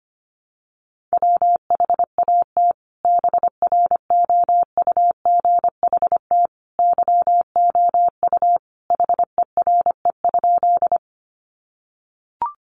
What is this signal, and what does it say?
Morse code, heard as a keyed tone, about 25 wpm, 700 Hz. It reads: WHAT BROUGHT YOU HERE?
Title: Sentences from Top 500 Words 25wpm